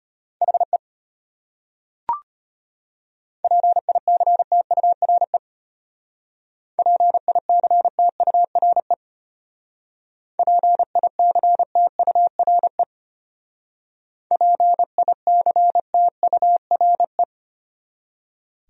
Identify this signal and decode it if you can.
Morse code, heard as a keyed tone, about 33 wpm, 700 Hz. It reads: HE PICTURE PICTURE PICTURE PICTURE